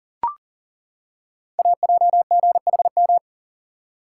Morse code, heard as a keyed tone, about 39 wpm, 690 Hz. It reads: AJGHM
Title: Code Groups - 5 Letters - 40wpm